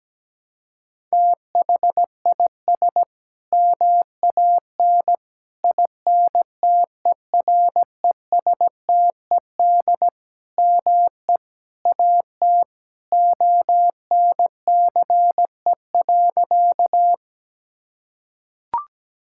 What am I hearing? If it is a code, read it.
Morse code, heard as a keyed tone, about 17 wpm, 700 Hz. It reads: THIS MAN INTERESTED ME AT ONCE.
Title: Sentences from Top 1000 Words - Review Part 6 - 17wpm